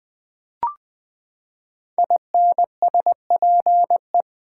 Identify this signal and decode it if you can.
Morse code, heard as a keyed tone, about 20 wpm, 700 Hz. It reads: INSPE
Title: Build A Word - 5 Letters At A Time - 10-Letter Words - 20wpm